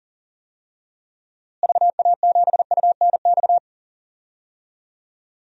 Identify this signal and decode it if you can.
Morse code, heard as a keyed tone, about 40 wpm, 700 Hz. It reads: VA7UNX
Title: DXCC-100 Call Signs - 6 Characters - Repeated 2x - 40wpm